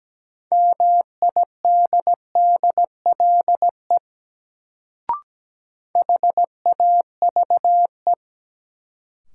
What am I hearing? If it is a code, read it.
Morse code, heard as a keyed tone, about 17 wpm, 700 Hz. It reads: MIDDLE HAVE
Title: Sets of 2 Words - New Word Drill - 17wpm